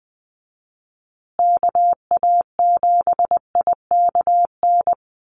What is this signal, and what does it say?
Morse code, heard as a keyed tone, about 20 wpm, 700 Hz. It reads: KA7IKN
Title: USA Call Signs Repeated 2x 20wpm